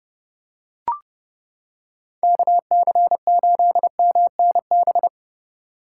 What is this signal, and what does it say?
Morse code, heard as a keyed tone, about 29 wpm, 700 Hz. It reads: KC8MNB